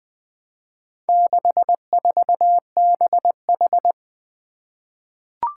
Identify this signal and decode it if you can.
Morse code, heard as a keyed tone, about 21 wpm, 700 Hz. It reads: 64BH
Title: Four Letter-Number Combinations - 20wpm